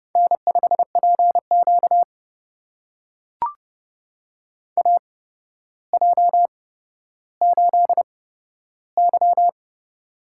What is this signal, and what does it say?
Morse code, heard as a keyed tone, about 30 wpm, spaced as for 13 wpm, 700 Hz. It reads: N5PQ AJ8Y
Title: USA Call Signs - 4 Characters - 8x Character Spacing - 30wpm